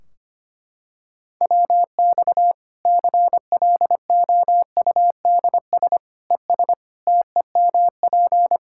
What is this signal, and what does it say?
Morse code, heard as a keyed tone, about 25 wpm, 700 Hz. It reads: WX CLOUDS ES TEMP
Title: Practice QSO - Exchange 2 - Info - 25wpm